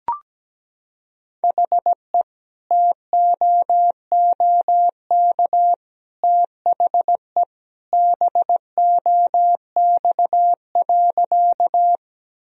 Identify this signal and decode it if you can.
Morse code, heard as a keyed tone, about 17 wpm, 700 Hz. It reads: HE TOOK THE BOX.
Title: Sentences from Top 1000 Words - Review Part 2 - 17wpm